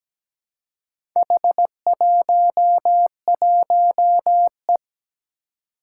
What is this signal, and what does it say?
Morse code, heard as a keyed tone, about 17 wpm, 700 Hz. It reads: H11E
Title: Four Challenging Letter-Number Combinations 17wpm